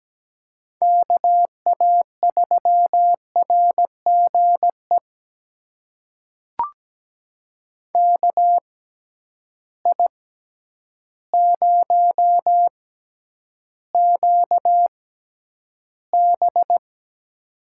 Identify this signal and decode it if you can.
Morse code, heard as a keyed tone, about 17 wpm, spaced as for 9 wpm, 700 Hz. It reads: KA3RGE KI0QB
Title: USA Call Signs - 6x Character Spacing - 17wpm